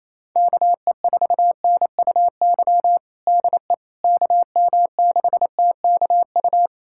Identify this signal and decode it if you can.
Morse code, heard as a keyed tone, about 28 wpm, 700 Hz. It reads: KE4NUY DE KM6TKU
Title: USA Callsign DE Callsign 28wpm